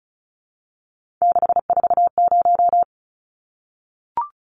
Morse code, heard as a keyed tone, about 34 wpm, 700 Hz. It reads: 640
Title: Numbers - 3-Digit 35wpm